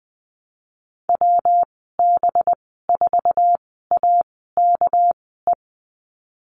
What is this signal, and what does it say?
Morse code, heard as a keyed tone, about 20 wpm, spaced as for 15 wpm, 700 Hz. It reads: WB4AKE
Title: 219 Lesson 35g - Number 0 - Call Signs 20wpm